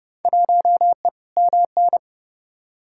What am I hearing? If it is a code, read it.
Morse code, heard as a keyed tone, about 30 wpm, 700 Hz. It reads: 1E MN
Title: Field Day Exchanges - 30wpm